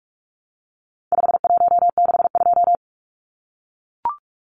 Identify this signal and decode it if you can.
Morse code, heard as a keyed tone, about 44 wpm, 700 Hz. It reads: 5162
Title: Numbers - 4-Digit 45wpm